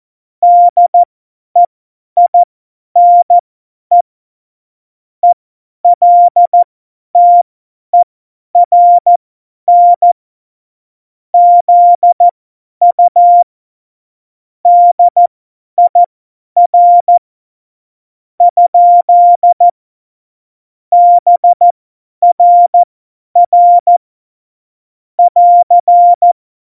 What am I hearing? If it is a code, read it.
Morse code, heard as a keyed tone, about 14 wpm, spaced as for 10 wpm, 700 Hz. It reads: DEINE ELTERN ZU DIR ? BRR <AR>
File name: Morsekurs / Witze / Witz 21 Juli 2020, 50 F70